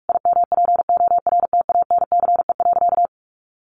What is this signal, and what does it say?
Morse code, heard as a keyed tone, about 44 wpm, 700 Hz. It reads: IMPORTANCE.
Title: Sentences from Top 1200 Words 45wpm